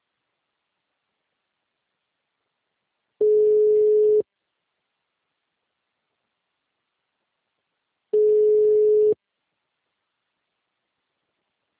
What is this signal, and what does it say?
Ringback tone (Europe, 425 Hz).
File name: out- -2001- - - .98